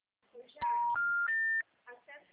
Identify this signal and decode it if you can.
Special information tone, with speech.